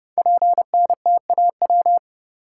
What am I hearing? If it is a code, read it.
Morse code, heard as a keyed tone, about 30 wpm, 700 Hz. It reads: PNTAW
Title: Code Groups - 5 Letters - 30wpm